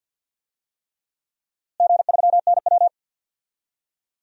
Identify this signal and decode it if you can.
Morse code, heard as a keyed tone, about 52 wpm, 700 Hz. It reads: G3NW
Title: DXCC-100 Call Signs - 4 Characters - Repeated 3x - 50wpm